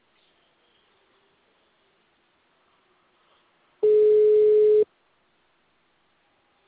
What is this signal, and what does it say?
Ringback tone (Europe, 425 Hz).